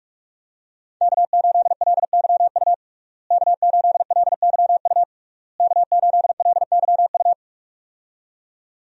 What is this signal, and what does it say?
Morse code, heard as a keyed tone, about 46 wpm, 700 Hz. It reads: K8LYU K8LYU K8LYU
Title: USA Call Signs - 5 Characters - Repeated 3x - 45wpm